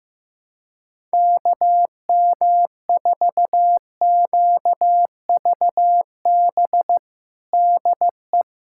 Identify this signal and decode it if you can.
Morse code, heard as a keyed tone, about 15 wpm, 700 Hz. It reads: KM4QVB DE
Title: USA Callsign DE Callsign - 6 Characters - 15wpm